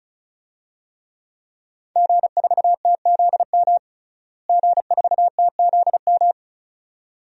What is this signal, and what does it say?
Morse code, heard as a keyed tone, about 34 wpm, 700 Hz. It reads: G4TZM G4TZM
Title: DXCC-100 Call Signs - 5 Characters - Repeated 2x - 35wpm